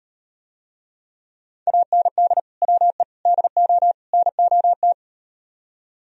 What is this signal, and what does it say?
Morse code, heard as a keyed tone, about 37 wpm, 700 Hz. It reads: AND WE DO NOT